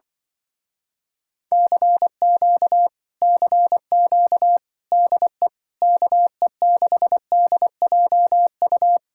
Morse code, heard as a keyed tone, about 24 wpm, 700 Hz. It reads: CQ CQ DE KE6DJU
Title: Speed-Racing Common QSO Phrases - 20wpm